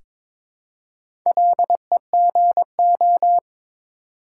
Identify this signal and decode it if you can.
Morse code, heard as a keyed tone, about 21 wpm, 700 Hz. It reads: LEGO
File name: practice / 22wpm / Difficult Words with Hints - 22wpm